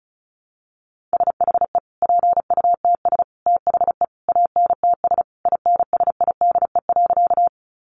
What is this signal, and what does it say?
Morse code, heard as a keyed tone, about 34 wpm, 700 Hz. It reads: SHE PUTS THE ANTS INSIDE.
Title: Additional Easy Phrases 35wpm